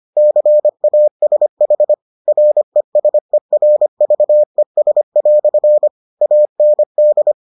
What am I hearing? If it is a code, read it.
Morse code, heard as a keyed tone, about 25 wpm, 600 Hz. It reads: CASH RESERVES" AND